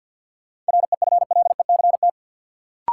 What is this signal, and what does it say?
Morse code, heard as a keyed tone, about 47 wpm, 700 Hz. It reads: REFLECT
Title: Sentences from Top 1500 Words - New Word Drill Alt - 50wpm